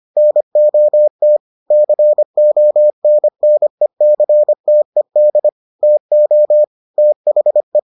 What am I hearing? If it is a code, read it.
Morse code, heard as a keyed tone, about 25 wpm, 600 Hz. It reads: NOT CONNECTED TO THE